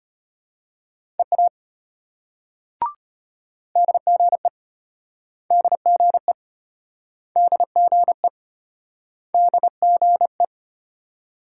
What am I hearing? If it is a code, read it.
Morse code, heard as a keyed tone, about 33 wpm, 700 Hz. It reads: EA DGE DGE DGE DGE